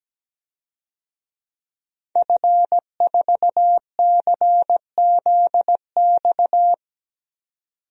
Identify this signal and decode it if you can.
Morse code, heard as a keyed tone, about 17 wpm, 700 Hz. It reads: F4CZX
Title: DXCC-100 Call Signs - 5 Characters - 17wpm